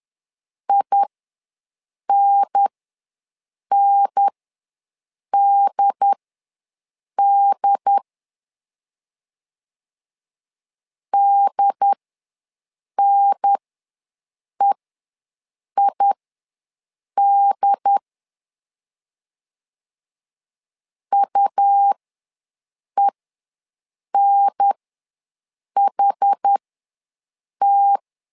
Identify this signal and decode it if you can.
Morse code, heard as a keyed tone, about 10 wpm, spaced as for 6 wpm, 800 Hz. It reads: INNDD DNEID UENHT
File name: LEZIONI / 2 UHDN / 2 UHDN ws 10wpm 2044